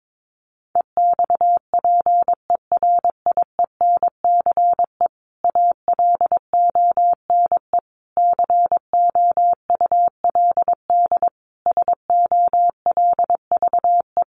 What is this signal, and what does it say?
Morse code, heard as a keyed tone, about 22 wpm, 700 Hz. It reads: EXPERIENCE ALONE COULD SOLVE
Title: Sentences from Top 1000 Words 22wpm